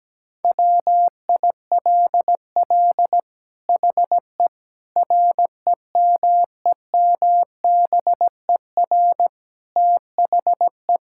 Morse code, heard as a keyed tone, about 17 wpm, 700 Hz. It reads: WILL HE REMEMBER THE
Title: Sentences from Top 1000 Words - Review Part 6 - 17wpm